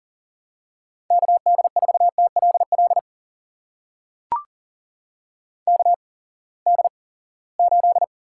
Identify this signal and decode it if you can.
Morse code, heard as a keyed tone, about 42 wpm, spaced as for 23 wpm, 700 Hz. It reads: KD4TLL KD8
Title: USA Call Signs - 6 Characters - 8x Character Spacing - 40wpm